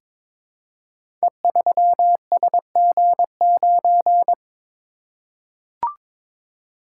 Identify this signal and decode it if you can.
Morse code, heard as a keyed tone, about 22 wpm, 700 Hz. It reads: E3SG9
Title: Code Groups - 5 Letter:Numbers - 22wpm